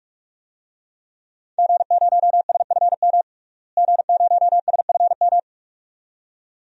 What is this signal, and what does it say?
Morse code, heard as a keyed tone, about 46 wpm, 700 Hz. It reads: G0SFM G0SFM
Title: DXCC-100 Call Signs - 5 Characters - Repeated 2x - 45wpm